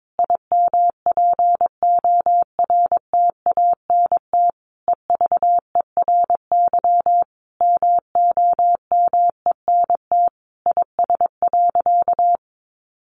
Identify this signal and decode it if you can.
Morse code, heard as a keyed tone, about 22 wpm, 700 Hz. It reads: IMPORTANT EVERY MOMENT IS.
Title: Sentences from Top 1000 Words 22wpm